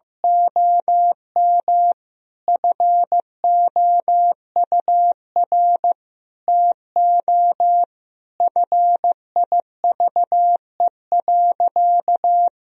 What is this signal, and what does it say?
Morse code, heard as a keyed tone, about 16 wpm, 700 Hz. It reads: OM FOUR TO FIVE.
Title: Sentences from Top 1000 Words - Review Part 2 - 15wpm